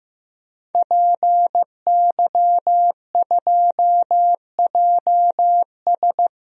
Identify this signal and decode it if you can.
Morse code, heard as a keyed tone, about 16 wpm, 700 Hz. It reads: PY2JS
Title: DXCC-100 Call Signs - 5 Characters - 15wpm